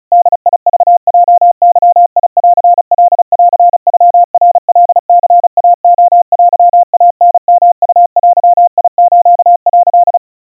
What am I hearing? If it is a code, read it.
Morse code (Japanese Wabun code), heard as a keyed tone, about 35 wpm, 700 Hz. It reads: ボクヲゲンカンノナカニイレテイタヨウデス」
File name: kawa / Morse sample / kamyu 130 cpm 700 Hz Crispy